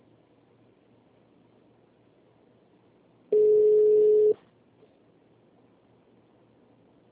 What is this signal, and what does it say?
Ringback tone (Europe, 425 Hz).